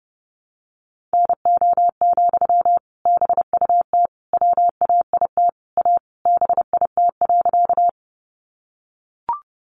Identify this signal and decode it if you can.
Morse code, heard as a keyed tone, about 30 wpm, 700 Hz. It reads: NO, BUT WAIT A BIT.
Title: Sentences from Top 900 Words 30wpm